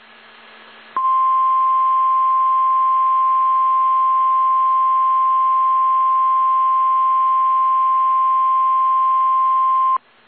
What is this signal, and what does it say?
Emergency Alert System (SAME) tones (the 1050 Hz attention tone). It is heard from 0.9 to 10.0 s.